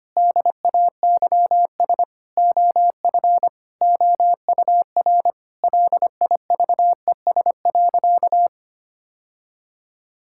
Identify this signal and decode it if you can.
Morse code, heard as a keyed tone, about 25 wpm, 700 Hz. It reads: DAYS OF OUR LIVES.
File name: practice / 25wpm / Sentences with 2-Word Phrases - 25wpm